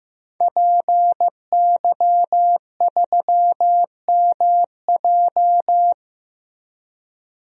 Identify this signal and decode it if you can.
Morse code, heard as a keyed tone, about 15 wpm, 700 Hz. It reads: PY3MJ
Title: DXCC-100 Call Signs - 5 Characters - Repeated 2x - 15wpm